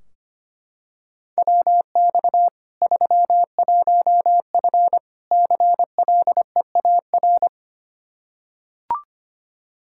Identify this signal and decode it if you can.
Morse code, heard as a keyed tone, about 25 wpm, 700 Hz. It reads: WX 31F CLEAR
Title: Practice QSO - Exchange 2 - Info - 25wpm